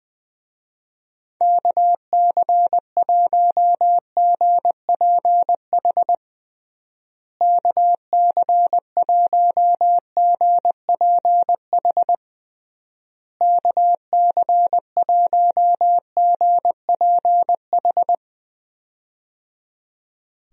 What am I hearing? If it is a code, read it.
Morse code, heard as a keyed tone, about 20 wpm, 700 Hz. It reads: KC1GPH KC1GPH KC1GPH
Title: USA Call Signs - 6 Characters - Repeated 3x - 20wpm